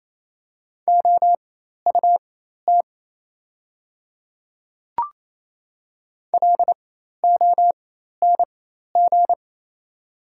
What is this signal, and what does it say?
Morse code, heard as a keyed tone, about 28 wpm, spaced as for 13 wpm, 700 Hz. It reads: OUT LONG
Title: 4x Character Spacing - Top 100 Words - 28wpm